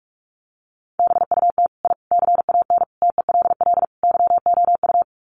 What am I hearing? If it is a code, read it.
Morse code, heard as a keyed tone, about 44 wpm, 700 Hz. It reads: BUT I CAN TELL YOU
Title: Sets of 5 Words Encore 45wpm